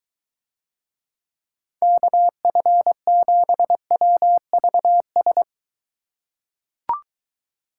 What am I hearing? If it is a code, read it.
Morse code, heard as a keyed tone, about 23 wpm, 700 Hz. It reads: KF7WVS